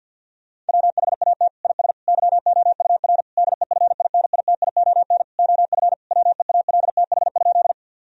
Morse code, heard as a keyed tone, about 50 wpm, 700 Hz. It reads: WHAT IS YOUR DEFINITION OF WEALTH?
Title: Sentences from Top 1000 Words - Review Part 4 - 50wpm